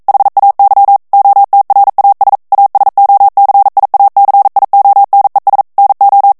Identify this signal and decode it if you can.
Morse code, heard as a keyed tone, about 42 wpm, 800 Hz. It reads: HAY OTRAS ASOCIACIONES NO
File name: morse mp3 / ESPAGNOL TEXTE EN CLAIR 1 VITESSE 40 MOTS MINUTES